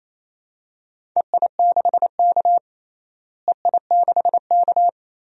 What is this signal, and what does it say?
Morse code, heard as a keyed tone, about 28 wpm, 700 Hz. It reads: EI6K EI6K
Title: DXCC-100 Call Signs - 4 Characters - Repeated 2x - 28wpm